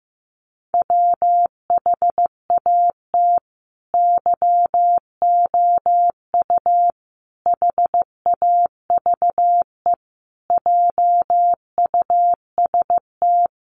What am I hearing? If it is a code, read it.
Morse code, heard as a keyed tone, about 15 wpm, 700 Hz. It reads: WHAT YOU HAVE JUST
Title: Sentences from Top 200 Words 15wpm